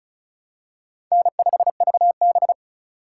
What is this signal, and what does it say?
Morse code, heard as a keyed tone, about 36 wpm, 700 Hz. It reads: N5VB
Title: USA Call Signs - 4 Characters - 6x Character Spacing - 35wpm